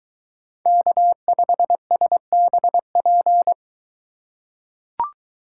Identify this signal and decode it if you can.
Morse code, heard as a keyed tone, about 23 wpm, 700 Hz. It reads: K5SBP